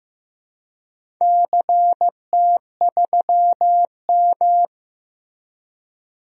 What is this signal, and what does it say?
Morse code, heard as a keyed tone, about 15 wpm, 700 Hz. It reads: CT3M
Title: DXCC-100 Call Signs - 4 Characters - Repeated 2x - 15wpm